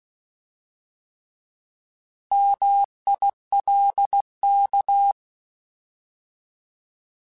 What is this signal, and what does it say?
Morse code, heard as a keyed tone, about 16 wpm, 800 Hz. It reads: MILK
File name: Alphabetical / Alphabetical mp3s / Ms @15